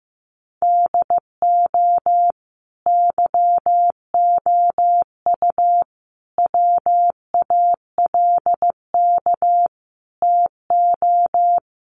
Morse code, heard as a keyed tone, about 15 wpm, 700 Hz. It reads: DO YOU WALK TO
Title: Sentences from Top 1400 Words 15wpm